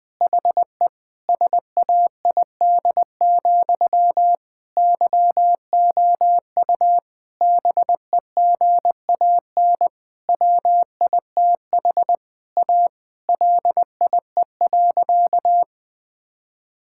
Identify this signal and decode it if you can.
Morse code, heard as a keyed tone, about 20 wpm, 700 Hz. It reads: HE SAID, YOU BEGAN WITH A LIE.
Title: Sentences from Top 1600 Words 20wpm